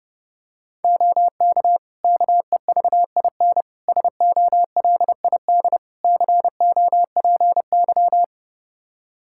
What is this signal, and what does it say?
Morse code, heard as a keyed tone, about 30 wpm, 700 Hz. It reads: OK KEVIN SOLID COPY
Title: Practice QSO - Exchange 3 - More Info - 30wpm